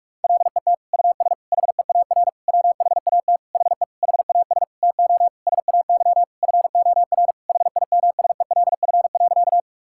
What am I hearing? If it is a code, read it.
Morse code, heard as a keyed tone, about 43 wpm, 700 Hz. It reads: LET US HEAR WHAT HE HAS TO SAY FOR HIMSELF.
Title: Sentences from Top 1600 Words 45wpm